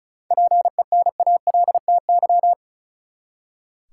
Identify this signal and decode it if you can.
Morse code, heard as a keyed tone, about 34 wpm, 700 Hz. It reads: PENALTY
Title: Seven Letter Words - 35wpm